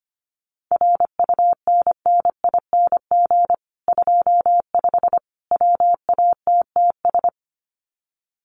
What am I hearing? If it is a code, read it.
Morse code, heard as a keyed tone, about 25 wpm, 700 Hz. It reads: RUNNING 25 WATTS
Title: Common QSO Phrases 25wpm